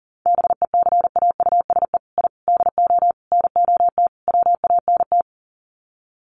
Morse code, heard as a keyed tone, about 40 wpm, 700 Hz. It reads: BECAUSE I DO NOT WANT
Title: Sets of 5 Words Encore 40wpm